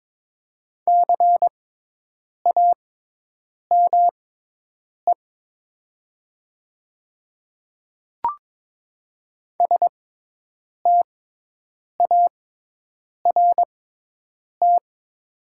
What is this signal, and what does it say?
Morse code, heard as a keyed tone, about 22 wpm, spaced as for 8 wpm, 700 Hz. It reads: CAME START
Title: 6x Character Spacing - Top 300 Words - 22wpm